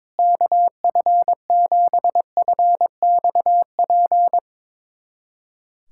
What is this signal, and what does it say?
Morse code, heard as a keyed tone, about 21 wpm, 700 Hz. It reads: KF7FXP
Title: USA Callsign DE Callsign - 6 Characters - 22wpm